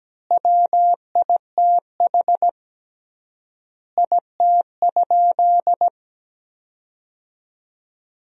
Morse code, heard as a keyed tone, about 17 wpm, 700 Hz. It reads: WITH IT?
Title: 3x Word-Spacing - Sentences from Top 100 Words - 17wpm